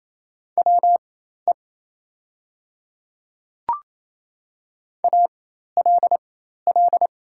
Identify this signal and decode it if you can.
Morse code, heard as a keyed tone, about 27 wpm, spaced as for 13 wpm, 690 Hz. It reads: WE ALL
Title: 4x Character Spacing - Top 300 Words - 28wpm